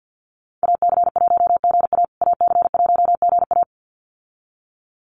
Morse code, heard as a keyed tone, about 50 wpm, 700 Hz. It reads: AC1GA AC1GA
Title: USA Call Signs Repeated 2x 50wpm